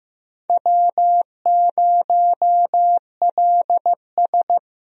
Morse code, heard as a keyed tone, about 15 wpm, 700 Hz. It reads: W0LS
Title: USA Call Signs - 4 Characters - Repeated 2x - 15wpm